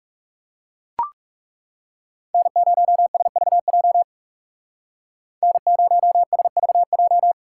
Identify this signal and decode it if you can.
Morse code, heard as a keyed tone, about 36 wpm, 700 Hz. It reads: N0SVJ N0SVJ
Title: Speed Racing - USA 1x3 Call Signs 30wpm